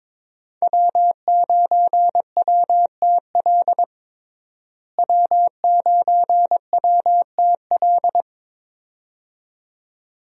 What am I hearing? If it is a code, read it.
Morse code, heard as a keyed tone, about 22 wpm, 700 Hz. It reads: W9WTL W9WTL
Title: USA Call Signs - 5 Characters - Repeated 2x - 22wpm